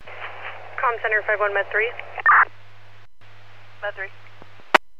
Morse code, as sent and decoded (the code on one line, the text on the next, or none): none